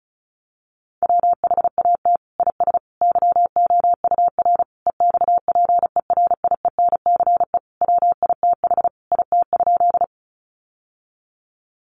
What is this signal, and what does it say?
Morse code, heard as a keyed tone, about 34 wpm, 700 Hz. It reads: WHAT IS YOUR EXPERIENCE WITH IT?
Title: Sentences from Top 600 Words 35wpm